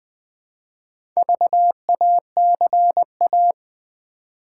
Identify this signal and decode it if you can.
Morse code, heard as a keyed tone, about 20 wpm, 700 Hz. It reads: VACA
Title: Build A Word - 2 Letters At A Time - 8-Letter Words - 20wpm